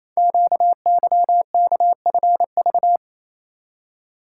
Morse code, heard as a keyed tone, about 27 wpm, 690 Hz. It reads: QYKFV
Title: Code Groups - 5 Letters - 28wpm